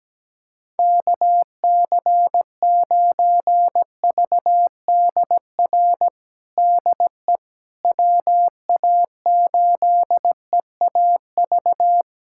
Morse code, heard as a keyed tone, about 17 wpm, 700 Hz. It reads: KC9VDR DE WA8EAV
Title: USA Callsign DE Callsign 17wpm